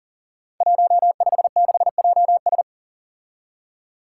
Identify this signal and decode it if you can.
Morse code, heard as a keyed tone, about 42 wpm, 700 Hz. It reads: 156JS
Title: Five Challenging Letter-Number Combinations 40wpm